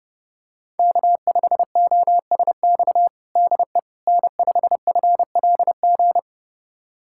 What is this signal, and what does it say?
Morse code, heard as a keyed tone, about 30 wpm, 700 Hz. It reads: K5OSX DE N5FLG
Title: USA Callsign DE Callsign - 5 Characters - 30wpm